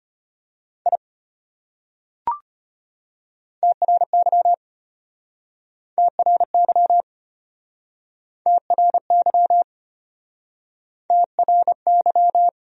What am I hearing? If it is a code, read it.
Morse code, heard as a keyed tone, about 31 wpm, 700 Hz. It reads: I TRY TRY TRY TRY
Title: Speed-Racing Top 300 Words 25wpm